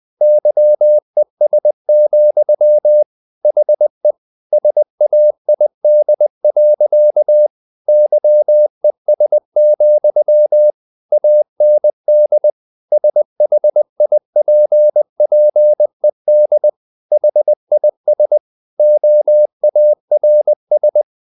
Morse code, heard as a keyed tone, about 20 wpm, 600 Hz. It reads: YES, HE SAID. YES, AND SHIPPED HIS OARS
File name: the old man and the sea / the old man and the sea part 1 20wpm